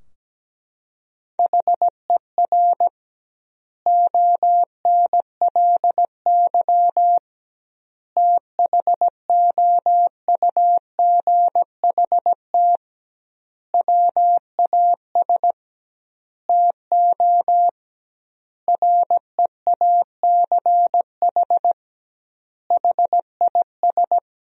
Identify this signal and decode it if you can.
Morse code, heard as a keyed tone, about 17 wpm, 700 Hz. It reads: HER ONLY THOUGHT WAS TO REACH HIS
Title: 2x Word-Spacing - Sentences from Top 400 Words - 17wpm